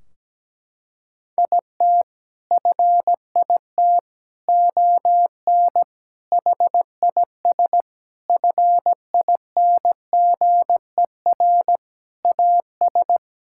Morse code, heard as a keyed tone, about 17 wpm, 700 Hz. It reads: IT FIT ON HIS FINGER AS
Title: Sentences from Top 1600 Words 17wpm